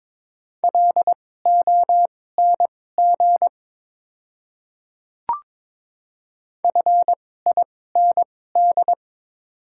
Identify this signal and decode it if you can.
Morse code, heard as a keyed tone, about 21 wpm, spaced as for 16 wpm, 700 Hz. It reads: LONG FIND
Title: 2x Character Spacing - Top 300 Words - 22wpm